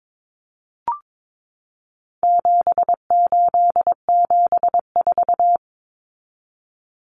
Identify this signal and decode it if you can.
Morse code, heard as a keyed tone, about 22 wpm, 700 Hz. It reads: 7874
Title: Common Cut Numbers - 4-Digit 22wpm